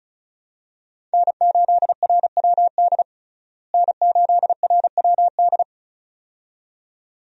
Morse code, heard as a keyed tone, about 34 wpm, 700 Hz. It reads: N8RWD N8RWD
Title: USA Call Signs - 5 Characters - Repeated 2x - 35wpm